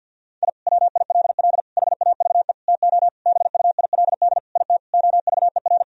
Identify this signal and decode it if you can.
Morse code, heard as a keyed tone, about 50 wpm, 700 Hz. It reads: I WILL HAVE TO BUILD IT OVER
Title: Sentences from Top 1000 Words - Review Part 1 - 50wpm